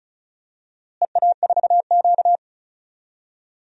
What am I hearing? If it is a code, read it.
Morse code, heard as a keyed tone, about 34 wpm, 700 Hz. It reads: EA4Q
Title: DXCC-100 Call Signs - 4 Characters - Repeated 2x - 35wpm